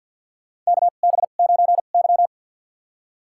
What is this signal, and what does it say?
Morse code, heard as a keyed tone, about 52 wpm, spaced as for 36 wpm, 700 Hz. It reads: KB9Y
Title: USA Call Signs - 2x Character Spacing - 50wpm